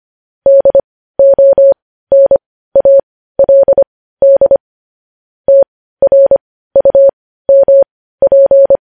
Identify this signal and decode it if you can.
Morse code, heard as a keyed tone, about 25 wpm, spaced as for 15 wpm, 550 Hz. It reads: DONALD TRUMP
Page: news headlines in morse code at 15 wpm